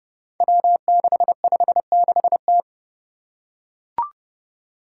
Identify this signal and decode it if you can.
Morse code, heard as a keyed tone, about 30 wpm, 700 Hz. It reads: W656T
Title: Five Challenging Letter-Number Combinations 30wpm